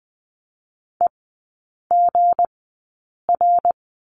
Morse code, heard as a keyed tone, about 20 wpm, spaced as for 8 wpm, 690 Hz. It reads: EGR
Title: 159 Lesson 26e - Letter g - Three Characters 20wpm